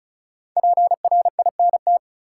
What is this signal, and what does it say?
Morse code, heard as a keyed tone, about 36 wpm, 700 Hz. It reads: PRINT
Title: Sentences from Top 1400 Words - New Word Drill Alt - 35wpm